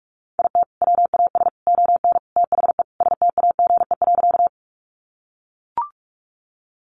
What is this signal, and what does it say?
Morse code, heard as a keyed tone, about 44 wpm, 700 Hz. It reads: IT WAS ON THE STAGE.
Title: Sentences from Top 1200 Words 45wpm